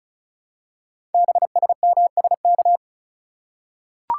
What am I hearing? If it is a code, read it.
Morse code, heard as a keyed tone, about 36 wpm, 700 Hz. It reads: BSMSK
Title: Code Groups - 5 Letters - 35wpm